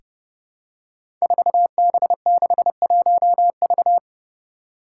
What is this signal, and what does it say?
Morse code, heard as a keyed tone, about 30 wpm, 700 Hz. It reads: 4B61V
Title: Five Challenging Letter-Number Combinations 30wpm